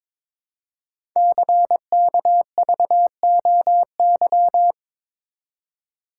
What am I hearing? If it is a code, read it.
Morse code, heard as a keyed tone, about 22 wpm, 700 Hz. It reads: CKVOY
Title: Code Groups - 5 Letters - 22wpm